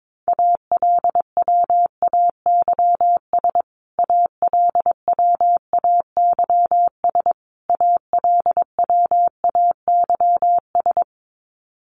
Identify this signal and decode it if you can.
Morse code, heard as a keyed tone, about 22 wpm, 700 Hz. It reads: ALWAYS ALWAYS ALWAYS
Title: Top 500 Words Repeated 22wpm